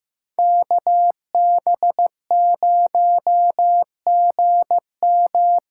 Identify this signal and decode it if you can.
Morse code, heard as a keyed tone, about 15 wpm, 700 Hz. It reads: KB0GM
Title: USA Call Signs - 6 Characters - 8x Character Spacing - 15wpm